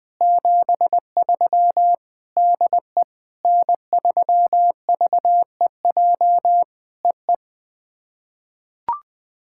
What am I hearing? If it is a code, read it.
Morse code, heard as a keyed tone, about 20 wpm, 700 Hz. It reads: 73 DE N3VEJ EE
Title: POTA QSO Practice - 20wpm